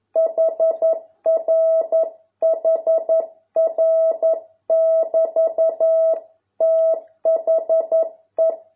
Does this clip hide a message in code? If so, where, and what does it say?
Yes: Morse code from 0.2 to 8.5 s, heard as a keyed tone, about 11 wpm, 650 Hz. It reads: HR HR <BT> THE